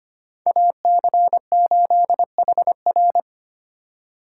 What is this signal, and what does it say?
Morse code, heard as a keyed tone, about 32 wpm, 700 Hz. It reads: AC8HR